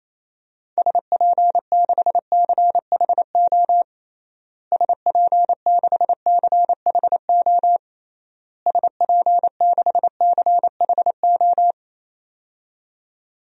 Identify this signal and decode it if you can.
Morse code, heard as a keyed tone, about 28 wpm, 700 Hz. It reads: SP6CHO SP6CHO SP6CHO
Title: DXCC-100 Call Signs - 6 Characters - Repeated 3x - 28wpm